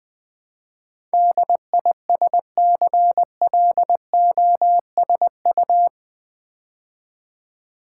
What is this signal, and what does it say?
Morse code, heard as a keyed tone, about 20 wpm, 700 Hz. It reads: DISCLOSU
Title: Build A Word - 4 Letters At A Time - 10-Letter Words - 20wpm